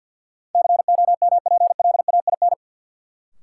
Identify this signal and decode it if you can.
Morse code, heard as a keyed tone, about 53 wpm, 700 Hz. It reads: COMPLAIN
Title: Sentences from Top 1300 Words - New Word Drill - 50wpm